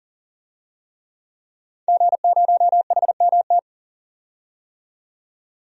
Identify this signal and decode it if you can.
Morse code, heard as a keyed tone, about 40 wpm, 700 Hz. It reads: G0HMT
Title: DXCC-100 Call Signs - 5 Characters - 40wpm